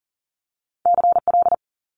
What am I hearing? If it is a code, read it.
Morse code, heard as a keyed tone, about 41 wpm, 700 Hz. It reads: CL
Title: Multi-letter Phonemes Quick Fire 40wpm